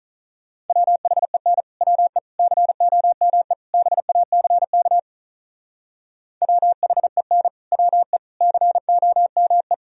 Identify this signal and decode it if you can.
Morse code, heard as a keyed tone, about 44 wpm, 700 Hz. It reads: WHEN WE COME BACK WHEN WE COME
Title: Speed-Racing - Sets of 4 Words 35wpm